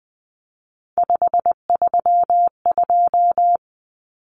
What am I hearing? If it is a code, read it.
Morse code, heard as a keyed tone, about 20 wpm, 700 Hz. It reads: 532
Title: Numbers - 3-Digit 20wpm